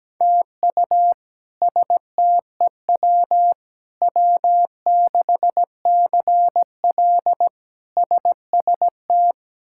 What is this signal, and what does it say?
Morse code, heard as a keyed tone, about 17 wpm, 700 Hz. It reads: TU STEW W6CL SST
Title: K1USN SST QSO Practice - 17wpm